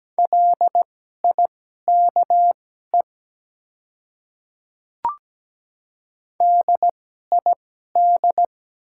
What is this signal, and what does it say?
Morse code, heard as a keyed tone, about 17 wpm, spaced as for 12 wpm, 700 Hz. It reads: LIKE DID
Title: 2x Character Spacing - Top 100 Words - 17wpm